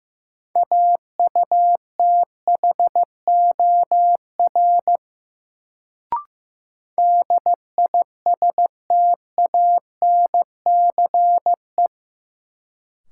Morse code, heard as a keyed tone, about 15 wpm, 700 Hz. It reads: AUTHOR DISTANCE
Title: Sentences from Top 1200 Words - New Word Drill - 15wpm